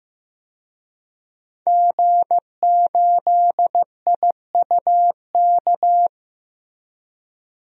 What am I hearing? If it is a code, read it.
Morse code, heard as a keyed tone, about 16 wpm, 700 Hz. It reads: G8IUK
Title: DXCC-100 Call Signs - 5 Characters - Repeated 3x - 15wpm